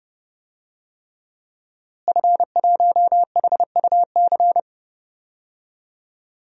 Morse code, heard as a keyed tone, about 30 wpm, 700 Hz. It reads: F1HUC
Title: DXCC-100 Call Signs - 5 Characters - Repeated 3x - 30wpm